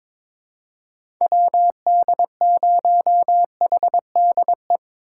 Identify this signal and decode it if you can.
Morse code, heard as a keyed tone, about 22 wpm, 700 Hz. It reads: WD0HDE
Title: USA Call Signs - 3x Character Spacing - 22wpm